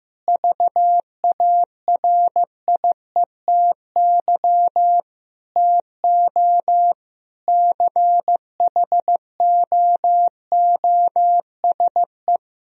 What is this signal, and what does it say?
Morse code, heard as a keyed tone, about 15 wpm, 700 Hz. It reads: VARIETY TO CHOOSE
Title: Sentences from Top 1000 Words - Review Part 4 - 15wpm